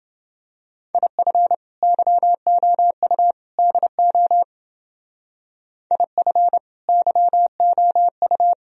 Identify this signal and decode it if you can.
Morse code, heard as a keyed tone, about 29 wpm, 700 Hz. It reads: IF YOU DO IF YOU